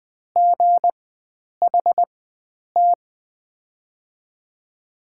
Morse code, heard as a keyed tone, about 21 wpm, spaced as for 9 wpm, 700 Hz. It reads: GHT
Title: 4x Character Spacing - Top 500 Words - 20wpm